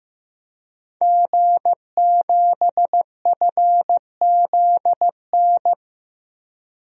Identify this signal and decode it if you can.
Morse code, heard as a keyed tone, about 15 wpm, 700 Hz. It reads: G7FZN
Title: DXCC-100 Call Signs - 5 Characters - Repeated 2x - 15wpm